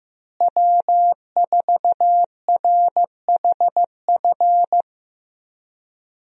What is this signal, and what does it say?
Morse code, heard as a keyed tone, about 15 wpm, 700 Hz. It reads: W4RHF
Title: USA Call Signs - 5 Characters - Repeated 3x - 15wpm